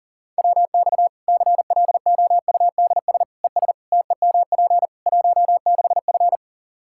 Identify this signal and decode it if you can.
Morse code, heard as a keyed tone, about 40 wpm, 700 Hz. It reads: WX CLOUDS ES TEMP 16F
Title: Practice QSO - Exchange 2 - Info - 40wpm